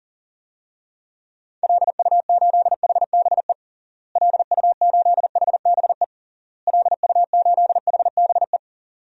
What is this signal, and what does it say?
Morse code, heard as a keyed tone, about 40 wpm, 700 Hz. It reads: LU8HBE LU8HBE LU8HBE
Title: DXCC-100 Call Signs - 6 Characters - Repeated 3x - 40wpm